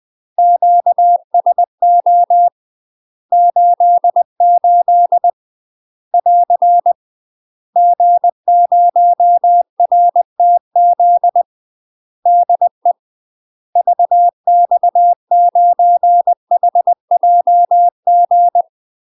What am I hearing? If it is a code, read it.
Morse code, heard as a keyed tone, about 20 wpm, 700 Hz. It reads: QSO 88 <AR> G0RTZ DE VX9HJG
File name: UK MASTER QSOs TRAINING / 20WPM / UK MASTER QSOs 0043